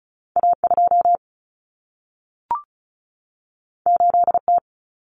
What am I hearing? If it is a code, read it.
Morse code, heard as a keyed tone, about 36 wpm, 700 Hz. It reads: A2 8T
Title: Common Cut Numbers - 2-Digit 35wpm